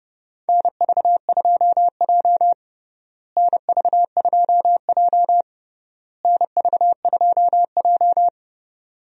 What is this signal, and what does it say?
Morse code, heard as a keyed tone, about 30 wpm, 700 Hz. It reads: NV2J NV2J NV2J
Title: USA Call Signs - 4 Characters - Repeated 3x - 30wpm